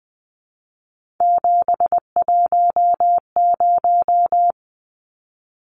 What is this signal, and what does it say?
Morse code, heard as a keyed tone, about 20 wpm, 700 Hz. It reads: 710
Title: Numbers - 3-Digit 20wpm